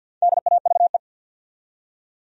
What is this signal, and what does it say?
Morse code, heard as a keyed tone, about 53 wpm, 700 Hz. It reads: DAVE
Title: USA Common Names - 50wpm